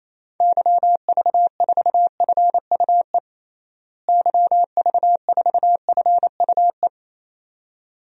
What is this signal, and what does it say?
Morse code, heard as a keyed tone, about 28 wpm, 700 Hz. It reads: YV4FUE YV4FUE
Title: DXCC-100 Call Signs - 6 Characters - Repeated 2x - 28wpm